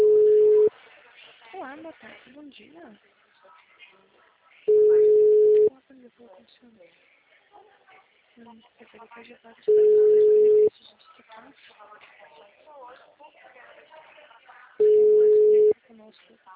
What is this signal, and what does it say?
Ringback tone (Europe, 425 Hz), with speech.